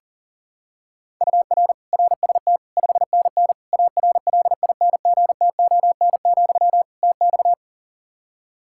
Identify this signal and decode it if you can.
Morse code, heard as a keyed tone, about 40 wpm, 700 Hz. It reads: UR RST 5NN ARLINGTON, TX
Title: Practice QSO - Exchange 1 - Beginning - 40wpm